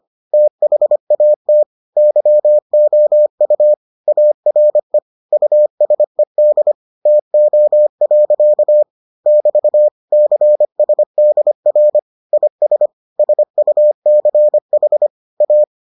Morse code, heard as a keyed tone, about 25 wpm, 600 Hz. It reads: THAT YOU ARE USED TO. <BT> CSDR IS SUCH A